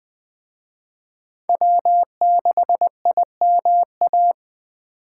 Morse code, heard as a keyed tone, about 20 wpm, 700 Hz. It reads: W6IMA
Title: Speed Racing - USA 1x3 Call Signs 17wpm